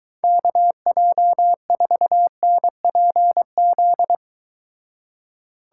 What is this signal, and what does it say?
Morse code, heard as a keyed tone, about 23 wpm, 700 Hz. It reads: KJ4NPZ